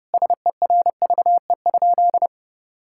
Morse code, heard as a keyed tone, about 32 wpm, 700 Hz. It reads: SERVE?
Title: Sentences from Top 1600 Words 30wpm